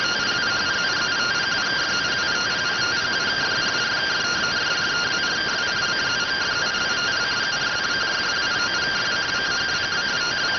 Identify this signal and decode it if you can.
Two-tone teleprinter signal, about 170 Hz shift.